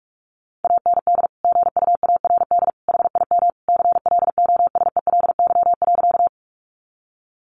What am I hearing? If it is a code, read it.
Morse code, heard as a keyed tone, about 44 wpm, 700 Hz. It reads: AND GUARD HIM CLOSELY.
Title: Sentences from Top 1400 Words 45wpm